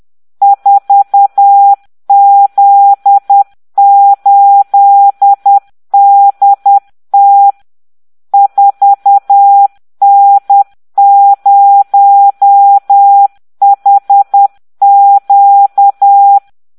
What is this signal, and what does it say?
Morse code, heard as a keyed tone, about 10 wpm, 800 Hz. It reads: 4Z8DT 4N0HQ